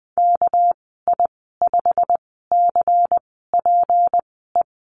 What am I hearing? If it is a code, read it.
Morse code, heard as a keyed tone, about 20 wpm, spaced as for 14 wpm, 700 Hz. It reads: KI5CPE
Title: 213 Lesson 34g - Number 8 - Call Signs 20wpm